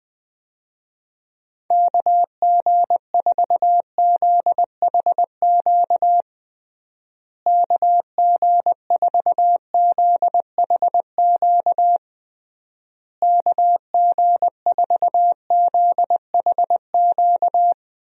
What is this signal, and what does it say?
Morse code, heard as a keyed tone, about 20 wpm, 700 Hz. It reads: KG4ZHQ KG4ZHQ KG4ZHQ
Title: USA Call Signs - 6 Characters - Repeated 3x - 20wpm